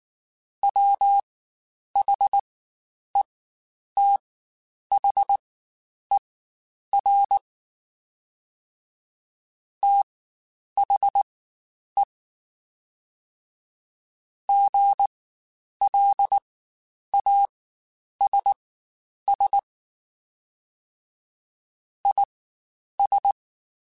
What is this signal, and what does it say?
Morse code, heard as a keyed tone, about 19 wpm, spaced as for 9 wpm, 800 Hz. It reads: WHETHER THE GLASS IS
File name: Spaced / Chuckles Spaced-mp3s / HalfFull sp @18